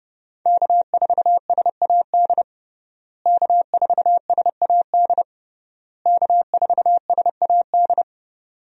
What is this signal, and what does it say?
Morse code, heard as a keyed tone, about 32 wpm, 700 Hz. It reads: K4SAD K4SAD K4SAD
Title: USA Call Signs - 5 Characters - Repeated 3x - 30wpm